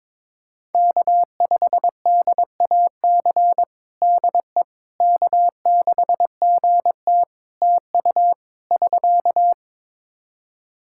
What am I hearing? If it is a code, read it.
Morse code, heard as a keyed tone, about 22 wpm, 700 Hz. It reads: K5DAC DE K6GT TU <SK>
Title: Practice QSO - Exchange 4 - Ending - 22wpm